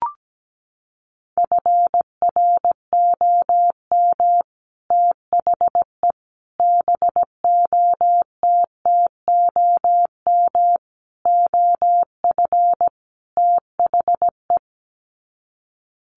Morse code, heard as a keyed tone, about 17 wpm, 700 Hz. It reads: FROM THE BOTTOM OF THE
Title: Sets of 5 Words Encore2 17wpm